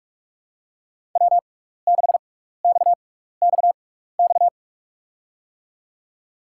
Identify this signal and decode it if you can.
Morse code, heard as a keyed tone, about 44 wpm, spaced as for 15 wpm, 700 Hz. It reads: W6XXX
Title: USA Call Signs - 6x Character Spacing - 45wpm